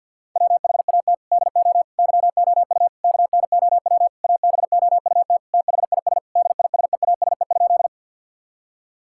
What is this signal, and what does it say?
Morse code, heard as a keyed tone, about 50 wpm, 700 Hz. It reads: WHAT DO YOU KNOW ABOUT THIS DISEASE?
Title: Sentences from Top 1000 Words - Review Part 6 - 50wpm